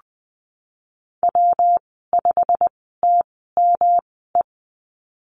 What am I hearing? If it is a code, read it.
Morse code, heard as a keyed tone, about 20 wpm, spaced as for 15 wpm, 700 Hz. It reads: W5TME
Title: 122 Lesson 20g - Number 3 - Call Signs 20wpm